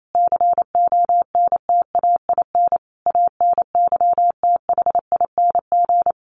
Morse code, heard as a keyed tone, about 28 wpm, 700 Hz. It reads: CONTAIN ANYTHING
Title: Sentences from Top 1100 Words 28wpm